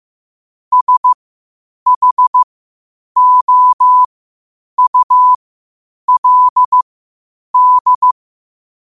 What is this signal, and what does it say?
Morse code, heard as a keyed tone, about 15 wpm, spaced as for 8 wpm, 1000 Hz. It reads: SHOULD
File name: qotdfiles / 8wpm